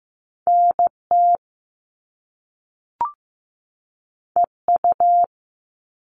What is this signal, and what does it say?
Morse code, heard as a keyed tone, about 15 wpm, 700 Hz. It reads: NT EU
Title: Top Two Letter Combinations in Words 15wpm